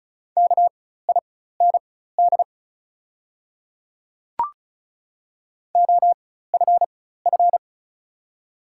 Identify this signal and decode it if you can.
Morse code, heard as a keyed tone, about 34 wpm, spaced as for 16 wpm, 690 Hz. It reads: KIND OFF
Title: 4x Character Spacing - Top 300 Words - 35wpm